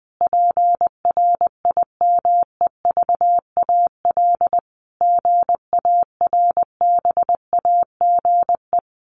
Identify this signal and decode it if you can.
Morse code, heard as a keyed tone, about 20 wpm, 700 Hz. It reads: PRIMEVAL GARBAGE
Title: A Sound Of Thunder 20wpm